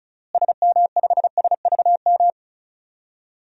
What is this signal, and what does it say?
Morse code, heard as a keyed tone, about 35 wpm, 700 Hz. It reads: SM5SVM
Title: DXCC-100 Call Signs - 6 Characters - Repeated 2x - 35wpm